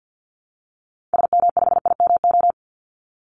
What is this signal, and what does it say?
Morse code, heard as a keyed tone, about 48 wpm, 700 Hz. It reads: SM5IMO
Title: Call Signs Repeated 3x - 50wpm